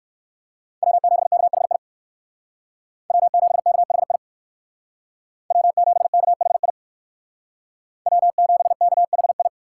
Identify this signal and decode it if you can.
Morse code, heard as a keyed tone, about 58 wpm, 700 Hz. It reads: W7KHI W7KHI W7KHI W7KHI